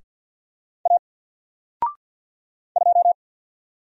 Morse code, heard as a keyed tone, about 49 wpm, 700 Hz. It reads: A 2
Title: Single Letter-Number Alt 50wpm